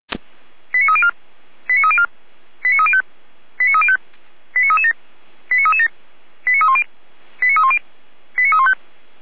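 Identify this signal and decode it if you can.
Five-tone selective call (ZVEI), heard as quick runs of five tones, tones of about 70 ms. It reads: 89304 89304 89305 89305 89337 89337 89310 89310 89315
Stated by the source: ZVEI five-tone selective call received by SDR.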